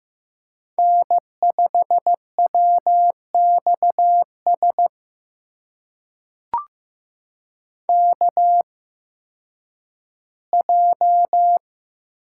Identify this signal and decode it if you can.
Morse code, heard as a keyed tone, about 15 wpm, spaced as for 10 wpm, 700 Hz. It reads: N5WXS KJ
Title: USA Call Signs - 5 Characters - 8x Character Spacing - 15wpm